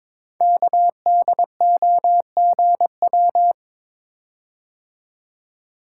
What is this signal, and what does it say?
Morse code, heard as a keyed tone, about 22 wpm, 700 Hz. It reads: KDOGW
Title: Code Groups - 5 Letters - 22wpm